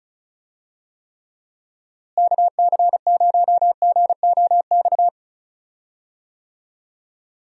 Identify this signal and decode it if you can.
Morse code, heard as a keyed tone, about 36 wpm, 700 Hz. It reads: KC0GOX
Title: USA Call Signs - 6 Characters - Repeated 3x - 35wpm